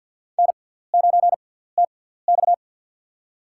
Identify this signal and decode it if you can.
Morse code, heard as a keyed tone, about 53 wpm, spaced as for 17 wpm, 700 Hz. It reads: N9TX
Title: USA Call Signs - 4 Characters - 6x Character Spacing - 50wpm